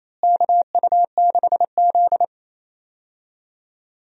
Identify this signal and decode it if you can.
Morse code, heard as a keyed tone, about 28 wpm, 700 Hz. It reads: KU6Z
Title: USA Call Signs - 4 Characters - Repeated 3x - 28wpm